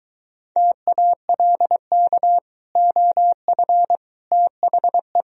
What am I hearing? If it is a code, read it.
Morse code, heard as a keyed tone, about 23 wpm, 700 Hz. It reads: TALK OF THE